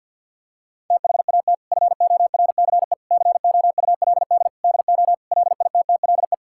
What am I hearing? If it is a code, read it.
Morse code, heard as a keyed tone, about 50 wpm, 700 Hz. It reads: THAT FORCE COULD DO LITTLE
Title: Sentences from Top 1000 Words - Review Part 5 - 50wpm